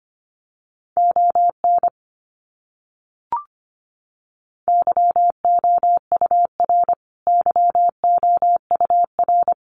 Morse code, heard as a keyed tone, about 25 wpm, 700 Hz. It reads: ON YOUR YOUR
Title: Top 200 Words Repeated 25wpm